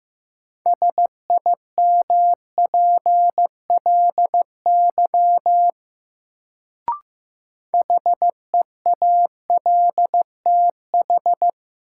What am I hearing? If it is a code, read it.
Morse code, heard as a keyed tone, about 15 wpm, 700 Hz. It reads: SIMPLY HEALTH
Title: Sentences from Top 600 Words - New Word Drill - 15wpm